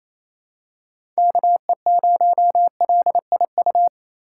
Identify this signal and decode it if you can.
Morse code, heard as a keyed tone, about 28 wpm, 700 Hz. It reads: KE0LIU
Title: USA Call Signs - 6x Character Spacing - 28wpm